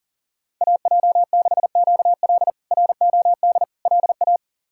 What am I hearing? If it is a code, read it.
Morse code, heard as a keyed tone, about 40 wpm, 700 Hz. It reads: AJ6QL ROD LA
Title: K1USN SST QSO Practice - 40wpm